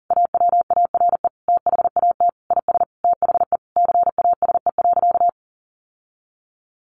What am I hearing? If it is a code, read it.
Morse code, heard as a keyed tone, about 40 wpm, 700 Hz. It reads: AWARE THAT IS THE CASE.
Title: Sentences from Top 1200 Words 40wpm